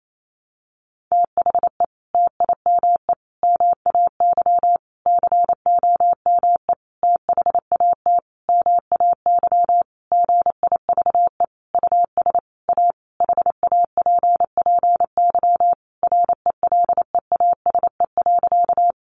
Morse code, heard as a keyed tone, about 28 wpm, 700 Hz. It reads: THE TIME MAY COME THAT MAY GIVE US A HAPPY RELEASE.
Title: Sentences from Top 1100 Words 28wpm